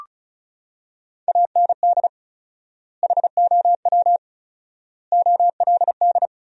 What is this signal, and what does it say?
Morse code, heard as a keyed tone, about 34 wpm, 700 Hz. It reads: AND HOW OLD
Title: 4x Word-Spacing - Sentences from Top 200 Words - 35wpm